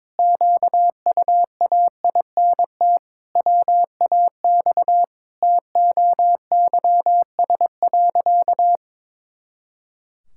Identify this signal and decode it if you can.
Morse code, heard as a keyed tone, about 22 wpm, 700 Hz. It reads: QUAINT WAX TOYS.
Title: Pangram Challenge v2 - 22wpm